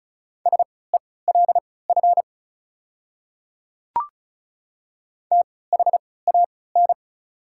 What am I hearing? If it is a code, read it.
Morse code, heard as a keyed tone, about 36 wpm, spaced as for 20 wpm, 700 Hz. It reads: SELF THAN
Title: 3x Character Spacing - Top 200 Words - 35wpm